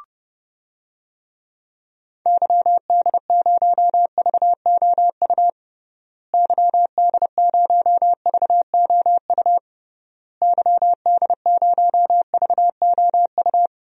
Morse code, heard as a keyed tone, about 30 wpm, 700 Hz. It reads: YD0VOU YD0VOU YD0VOU
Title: DXCC-100 Call Signs - 6 Characters - Repeated 3x - 30wpm